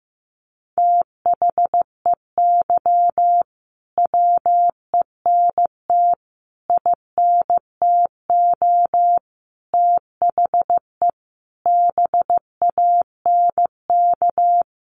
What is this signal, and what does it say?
Morse code, heard as a keyed tone, about 15 wpm, 700 Hz. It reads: THEY WENT INTO THE BANK
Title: Sentences from Top 900 Words 15wpm